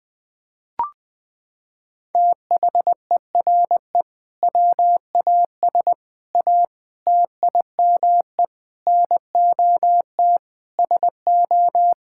Morse code, heard as a keyed tone, about 20 wpm, 700 Hz. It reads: THERE WAS A TIME NOT SO
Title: Sets of 8 Words - 20wpm